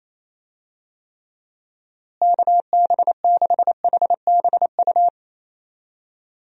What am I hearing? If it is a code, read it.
Morse code, heard as a keyed tone, about 28 wpm, 700 Hz. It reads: KB6HBU
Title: USA Call Signs - 6 Characters - Repeated 2x - 28wpm